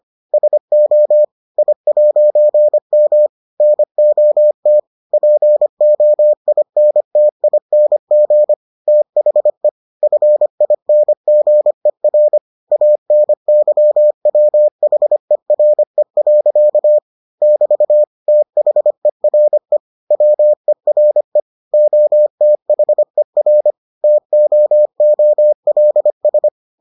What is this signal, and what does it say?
Morse code, heard as a keyed tone, about 25 wpm, 600 Hz. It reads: SO I'M NOT POINTING THE FINGER ANYWHERE. <BT> THERE WERE OTHER TOOLS